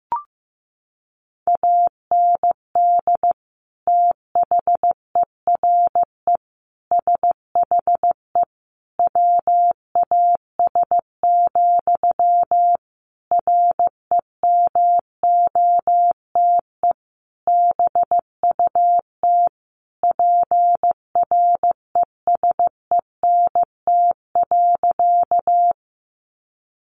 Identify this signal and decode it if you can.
Morse code, heard as a keyed tone, about 15 wpm, 700 Hz. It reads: AND THERE SHE WAS, REMOTE BUT PRESENT.
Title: Sentences from Top 1300 Words 15wpm